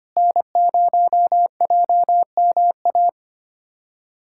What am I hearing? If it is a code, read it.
Morse code, heard as a keyed tone, about 25 wpm, 700 Hz. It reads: N0JMA
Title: USA Callsign DE Callsign 25wpm